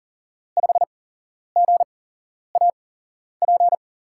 Morse code, heard as a keyed tone, about 40 wpm, 700 Hz. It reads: 5 G A P
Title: USA Call Signs - 5 Characters - 8x Character Spacing - 40wpm